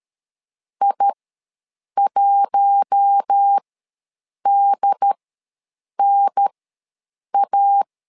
Morse code, heard as a keyed tone, about 13 wpm, spaced as for 7 wpm, 800 Hz. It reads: I1DNA
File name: LEZIONI / 4 Q1 ws 12wpm 2129